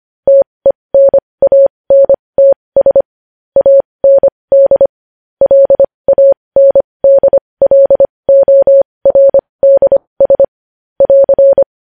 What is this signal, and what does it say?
Morse code, heard as a keyed tone, about 25 wpm, spaced as for 20 wpm, 550 Hz. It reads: TENANTS AND LANDLORDS <AR>
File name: podcast / News headlines-1x-25x20